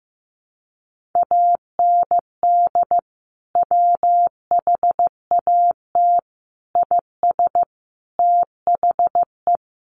Morse code, heard as a keyed tone, about 15 wpm, 700 Hz. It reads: AND WHAT IS THE
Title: Sentences from Top 1500 Words 15wpm